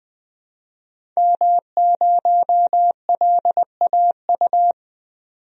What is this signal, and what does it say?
Morse code, heard as a keyed tone, about 20 wpm, 700 Hz. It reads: M0LAU
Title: DXCC-100 Call Signs - 5 Characters - 20wpm